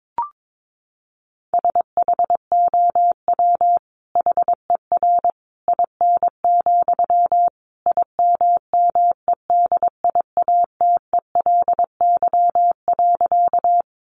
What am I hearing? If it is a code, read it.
Morse code, heard as a keyed tone, about 22 wpm, 700 Hz. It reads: SHOW HER IN, IMMEDIATELY.
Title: Sentences from Top 1100 Words 22wpm